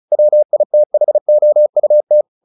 Morse code, heard as a keyed tone, about 34 wpm, 600 Hz. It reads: WITHOUT
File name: the old man and the sea 2 35wpm